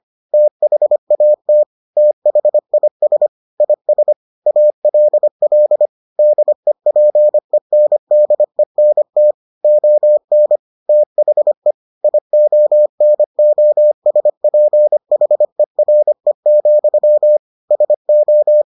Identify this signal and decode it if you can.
Morse code, heard as a keyed tone, about 25 wpm, 600 Hz. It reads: THAT THIS IS ALL DEPENDENT ON THE IONOSPHERE, SO